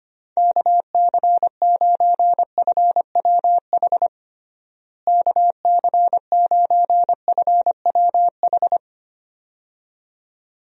Morse code, heard as a keyed tone, about 25 wpm, 700 Hz. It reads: KC9FWH KC9FWH
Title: USA Call Signs - 6 Characters - Repeated 2x - 25wpm